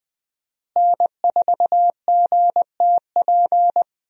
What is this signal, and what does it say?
Morse code, heard as a keyed tone, about 20 wpm, 700 Hz. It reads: N4GTP
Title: USA Call Signs - 5 Characters - Repeated 2x - 20wpm